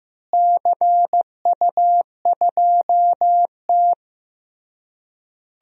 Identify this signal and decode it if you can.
Morse code, heard as a keyed tone, about 16 wpm, 700 Hz. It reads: CU2T
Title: DXCC-100 Call Signs - 4 Characters - Repeated 3x - 15wpm